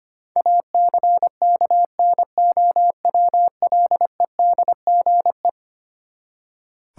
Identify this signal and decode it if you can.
Morse code, heard as a keyed tone, about 25 wpm, 700 Hz. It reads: ACKNOWLEDGE
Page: morse-code-ninja-practice-25wpm